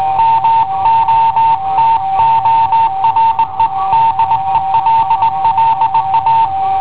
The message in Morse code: MOTORE DELLA